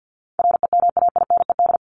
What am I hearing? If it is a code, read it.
Morse code, heard as a keyed tone, about 50 wpm, 700 Hz. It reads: REMAINED
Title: ED Suffix - Top 100 Words with Base Words 50wpm